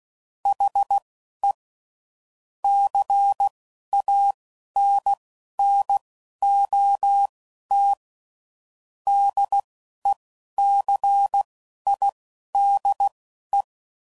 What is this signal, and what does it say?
Morse code, heard as a keyed tone, about 16 wpm, spaced as for 11 wpm, 800 Hz. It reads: HE CANNOT DECIDE